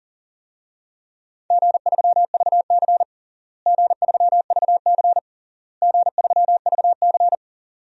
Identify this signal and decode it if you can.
Morse code, heard as a keyed tone, about 40 wpm, 700 Hz. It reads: G3VC G3VC G3VC
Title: DXCC-100 Call Signs - 4 Characters - Repeated 3x - 40wpm